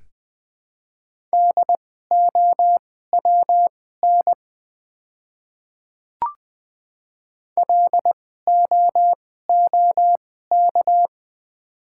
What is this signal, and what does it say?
Morse code, heard as a keyed tone, about 20 wpm, spaced as for 14 wpm, 700 Hz. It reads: DOWN LOOK
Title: 2x Character Spacing - Top 100 Words - 20wpm